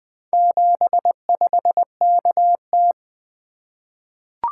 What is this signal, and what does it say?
Morse code, heard as a keyed tone, about 21 wpm, 700 Hz. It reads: 75KT
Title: Four Letter-Number Combinations - 20wpm